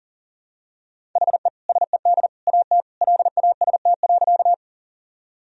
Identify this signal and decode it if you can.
Morse code, heard as a keyed tone, about 40 wpm, 700 Hz. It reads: HE SED AT LAST.
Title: Sentences from Top 300 Words and Abbreviations 40wpm